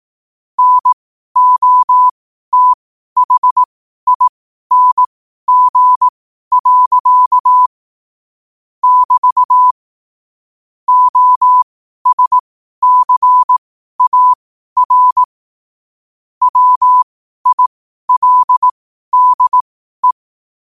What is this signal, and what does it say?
Morse code, heard as a keyed tone, about 18 wpm, spaced as for 12 wpm, 1000 Hz. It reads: NOTHING. <BT> OSCAR WILDE
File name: qotdfiles / 12wpm